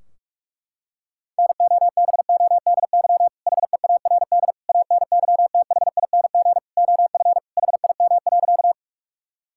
Morse code, heard as a keyed tone, about 41 wpm, 700 Hz. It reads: NOBODY HEARD ANYTHING OF HIM.
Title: Sentences from Top 1600 Words 45wpm